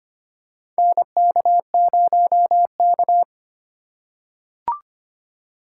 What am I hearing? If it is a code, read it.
Morse code, heard as a keyed tone, about 25 wpm, 700 Hz. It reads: NK0K
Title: USA Call Signs - 4 Characters - 3x Character Spacing - 25wpm